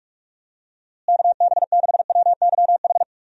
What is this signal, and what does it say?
Morse code, heard as a keyed tone, about 41 wpm, 700 Hz. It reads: KB6WYH
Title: USA Call Signs - 6 Characters - Repeated 3x - 45wpm